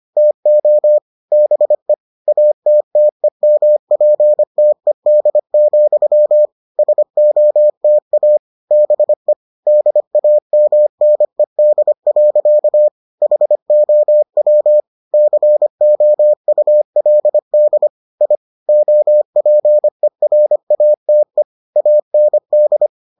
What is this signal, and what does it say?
Morse code, heard as a keyed tone, about 25 wpm, 600 Hz. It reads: TO BE ATTEMPTED, SOTA BE DAMNED. HOW COULD I OPERATE AND